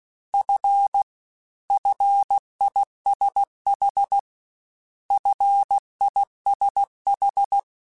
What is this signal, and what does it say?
Morse code, heard as a keyed tone, about 16 wpm, 800 Hz. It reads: F FISH FISH
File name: Single F mp3s / Single F Prefixes 3x2 Twice @15